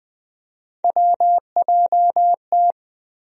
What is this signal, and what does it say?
Morse code, heard as a keyed tone, about 20 wpm, 700 Hz. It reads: WJT
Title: USA Call Sign Postfixes - 3 Letters 20wpm